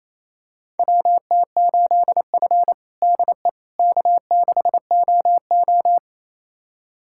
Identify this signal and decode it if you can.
Morse code, heard as a keyed tone, about 28 wpm, 700 Hz. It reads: WT8F DE K6OO
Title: USA Callsign DE Callsign - 4 Characters - 28wpm